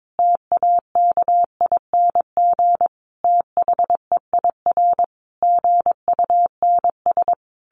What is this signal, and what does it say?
Morse code, heard as a keyed tone, about 22 wpm, 700 Hz. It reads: TAKING THEIR GUNS
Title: A Sound Of Thunder 22wpm